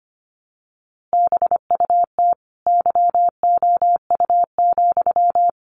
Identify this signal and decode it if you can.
Morse code, heard as a keyed tone, about 25 wpm, 700 Hz. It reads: BUT YOU,
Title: A Sound Of Thunder 25wpm